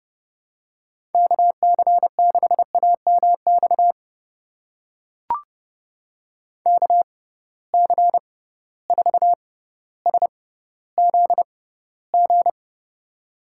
Morse code, heard as a keyed tone, about 30 wpm, spaced as for 15 wpm, 700 Hz. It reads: KC6AMX KC4SZG
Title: USA Call Signs - 6 Characters - 6x Character Spacing - 30wpm